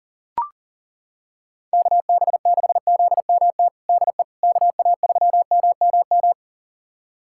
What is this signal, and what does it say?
Morse code, heard as a keyed tone, about 40 wpm, 700 Hz. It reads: KB6ZMT DE KA3MMM
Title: USA Callsign DE Callsign 40wpm